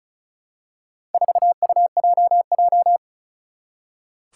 Morse code, heard as a keyed tone, about 36 wpm, 700 Hz. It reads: 4UJJ
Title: Four Challenging Letter-Number Combinations 35wpm